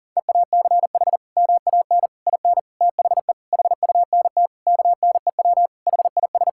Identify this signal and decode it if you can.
Morse code, heard as a keyed tone, about 40 wpm, 700 Hz. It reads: EACH MAN IN THE HUNT KNEW HIS
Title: Sentences from Top 1600 Words 40wpm